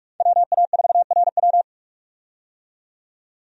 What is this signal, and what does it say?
Morse code, heard as a keyed tone, about 46 wpm, 700 Hz. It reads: WA4RW
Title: USA Call Signs - 5 Characters - 45wpm